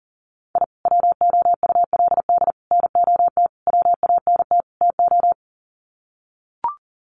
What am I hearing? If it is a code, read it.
Morse code, heard as a keyed tone, about 40 wpm, 700 Hz. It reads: I WOULD NOT WANT TO
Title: Sets of 5 Words 40wpm